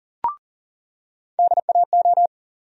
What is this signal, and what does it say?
Morse code, heard as a keyed tone, about 39 wpm, 700 Hz. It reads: DAO
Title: USA Call Sign Postfixes - 3 Letters 40wpm